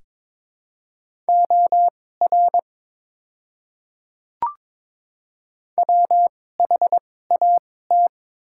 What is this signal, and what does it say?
Morse code, heard as a keyed tone, about 22 wpm, spaced as for 16 wpm, 700 Hz. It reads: OR WHAT
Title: 2x Character Spacing - Top 300 Words - 22wpm